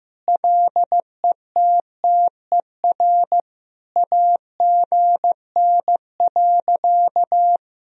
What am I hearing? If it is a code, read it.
Morse code, heard as a keyed tone, about 15 wpm, 700 Hz. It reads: LETTER AGN.
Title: Sentences from Top 300 Words and Abbreviations 15wpm